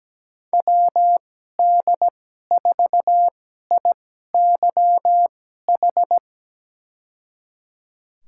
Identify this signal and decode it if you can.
Morse code, heard as a keyed tone, about 17 wpm, spaced as for 12 wpm, 700 Hz. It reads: WD4IYH
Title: USA Call Signs - 2x Character Spacing - 17wpm